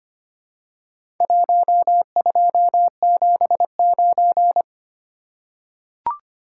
Morse code, heard as a keyed tone, about 25 wpm, 700 Hz. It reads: 1279
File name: practice / 25wpm / Four Letter-Number Combinations - 25wpm